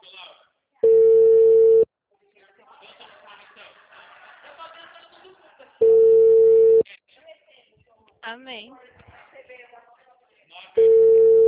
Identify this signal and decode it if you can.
Ringback tone (China, 450 Hz), with speech.